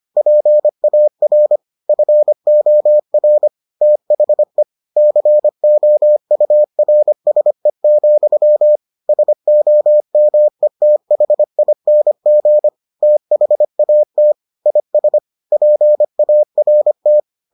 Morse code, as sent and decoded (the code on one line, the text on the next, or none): .--. .- .-. / ..-. --- .-. / - .... / -.-. --- ..- .-. ... --..-- / ... --- -- . - .... .. -. --. / - .... .- - / .. ... / .--. .- .-. -
PAR FOR THE COURSE, SOMETHING THAT IS PART